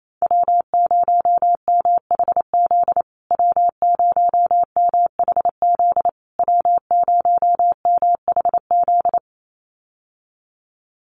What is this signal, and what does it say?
Morse code, heard as a keyed tone, about 28 wpm, 700 Hz. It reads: W0MHZ W0MHZ W0MHZ
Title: USA Call Signs Repeated 3x 28wpm